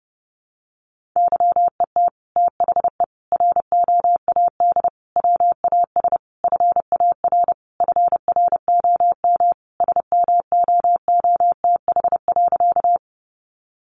Morse code, heard as a keyed tone, about 30 wpm, 700 Hz. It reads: YET THE ROAD WAS FAR FROM SMOOTH.
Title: Sentences from Top 1400 Words 30wpm